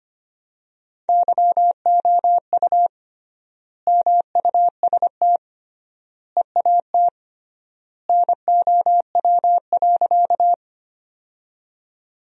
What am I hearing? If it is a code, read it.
Morse code, heard as a keyed tone, about 25 wpm, 700 Hz. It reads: YOU MUST EAT NOW.
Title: 3x Word-Spacing - Sentences from Top 300 Words - 25wpm